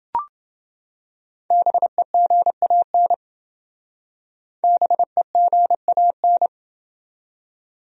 Morse code, heard as a keyed tone, about 28 wpm, 700 Hz. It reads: BEGAN BEGAN